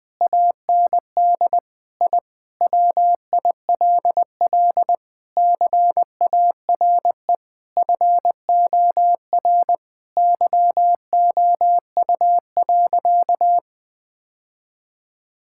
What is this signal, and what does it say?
Morse code, heard as a keyed tone, about 20 wpm, 700 Hz. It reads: AND I WILL CARE FOR YOU.
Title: Sentences from Top 1000 Words - Review Part 6 - 20wpm